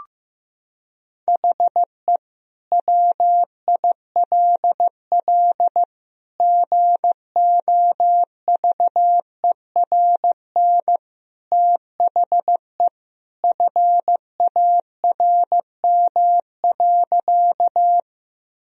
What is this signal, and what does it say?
Morse code, heard as a keyed tone, about 15 wpm, 700 Hz. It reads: HE WILL GOVERN THE FARM.
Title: Sentences from Top 1000 Words - Review Part 2 - 15wpm